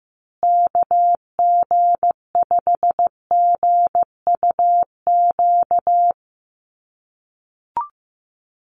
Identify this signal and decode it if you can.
Morse code, heard as a keyed tone, about 15 wpm, 700 Hz. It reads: KG5GUQ
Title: USA Call Signs Repeated 3x 15wpm